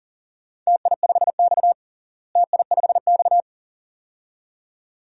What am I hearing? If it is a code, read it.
Morse code, heard as a keyed tone, about 40 wpm, 700 Hz. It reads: TI5X TI5X
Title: DXCC-100 Call Signs - 4 Characters - Repeated 2x - 40wpm